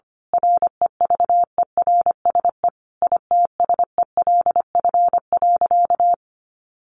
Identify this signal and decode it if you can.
Morse code, heard as a keyed tone, about 25 wpm, 700 Hz. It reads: REVERSE ITSELF.
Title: A Sound Of Thunder 25wpm